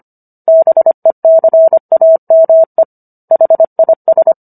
Morse code, heard as a keyed tone, about 25 wpm, 650 Hz. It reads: BECAME HIS